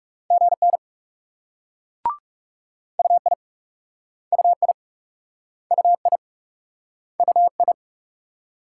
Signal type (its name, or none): Morse code